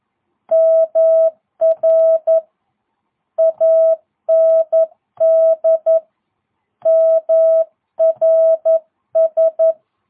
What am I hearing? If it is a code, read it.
Morse code, heard as a keyed tone, about 11 wpm, 660 Hz. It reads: MR AND MRS